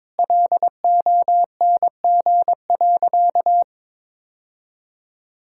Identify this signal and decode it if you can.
Morse code, heard as a keyed tone, about 22 wpm, 700 Hz. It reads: LONG.
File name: practice / 22wpm / Sentences with 3-Word Phrases - 22wpm